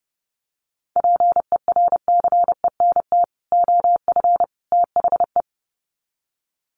Morse code, heard as a keyed tone, about 28 wpm, 700 Hz. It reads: PERCENT OF THE
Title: Sets of 3 Words 30wpm